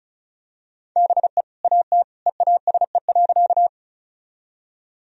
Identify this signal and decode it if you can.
Morse code, heard as a keyed tone, about 36 wpm, 700 Hz. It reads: BE AT EASE.
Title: Sentences from Top 1000 Words - Review Part 1 - 35wpm